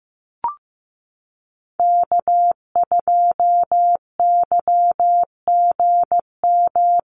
Morse code, heard as a keyed tone, about 15 wpm, 700 Hz. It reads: K2YGM
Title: Call Signs Repeated 2x - 15wpm